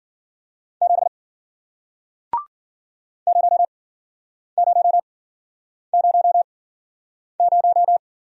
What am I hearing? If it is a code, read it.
Morse code, heard as a keyed tone, about 53 wpm, spaced as for 9 wpm, 700 Hz. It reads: 7 0000